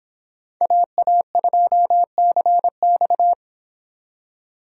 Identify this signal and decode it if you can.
Morse code, heard as a keyed tone, about 26 wpm, 700 Hz. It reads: AA2CX